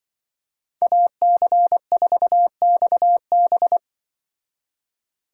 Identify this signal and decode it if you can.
Morse code, heard as a keyed tone, about 25 wpm, 700 Hz. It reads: AC4XB
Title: Speed Racing - USA Call Signs 20wpm